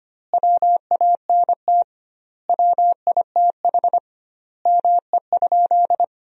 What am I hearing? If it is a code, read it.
Morse code, heard as a keyed tone, about 25 wpm, 700 Hz. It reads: WANT WITH ME?
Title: 2x Word-Spacing - Sentences from Top 200 Words - 25wpm